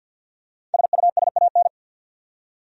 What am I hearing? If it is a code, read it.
Morse code, heard as a keyed tone, about 52 wpm, 700 Hz. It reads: SUSAN
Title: USA Common Names - Max 5 Letters - 50wpm